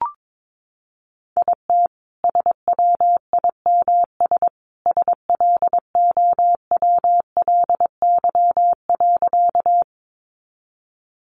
Morse code, heard as a keyed tone, about 22 wpm, 700 Hz. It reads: IT SWIMS SLOWLY.
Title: Additional Easy Phrases 22wpm